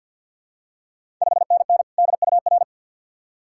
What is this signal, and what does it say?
Morse code, heard as a keyed tone, about 50 wpm, 690 Hz. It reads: 5NN DUR
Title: Ontario QSO Party Exchange 50wpm